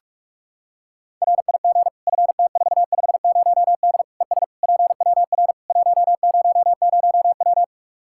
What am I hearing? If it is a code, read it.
Morse code, heard as a keyed tone, about 46 wpm, 700 Hz. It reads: RIG FT450D ES PWR 100W
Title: K1USN SST QSO Practice 45wpm